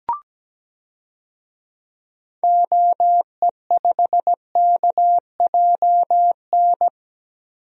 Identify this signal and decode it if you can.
Morse code, heard as a keyed tone, about 17 wpm, 700 Hz. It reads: OE5KJN
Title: DXCC-100 Call Signs - 6 Characters - Repeated 3x - 17wpm